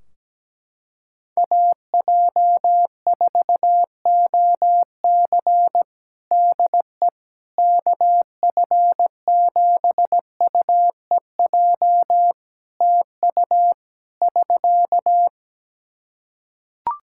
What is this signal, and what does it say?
Morse code, heard as a keyed tone, about 17 wpm, 700 Hz. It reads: AJ4OC DE KF7UEJ TU <SK>
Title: Practice QSO - Exchange 4 - Ending - 17wpm